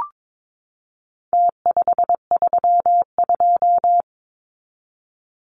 Morse code, heard as a keyed tone, about 22 wpm, 700 Hz. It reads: T532
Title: Numbers - Contest Style - 4-Digit 22wpm